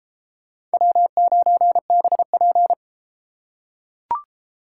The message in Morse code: W9BP

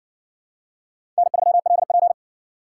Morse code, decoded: N3LP

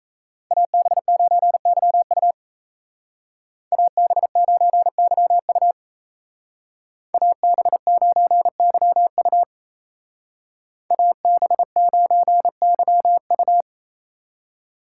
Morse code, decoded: AB9YU AB9YU AB9YU AB9YU

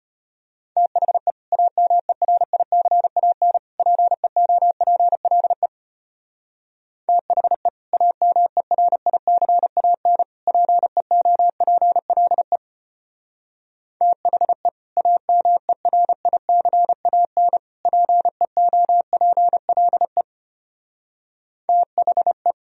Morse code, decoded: THE AMERICAN PEOPLE THE AMERICAN PEOPLE THE AMERICAN PEOPLE THE

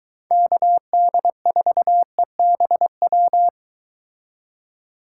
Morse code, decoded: KD4EBW

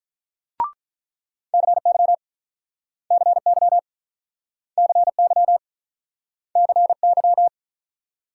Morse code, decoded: CY CY CY CY

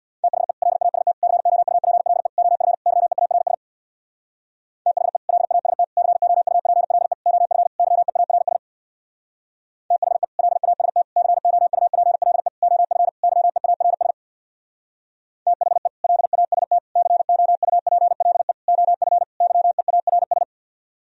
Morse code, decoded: THE LAST COUPLE OF YEARS THE LAST COUPLE OF YEARS THE LAST COUPLE OF YEARS THE LAST COUPLE OF YEARS

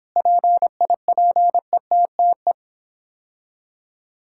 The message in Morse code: PIPETTE

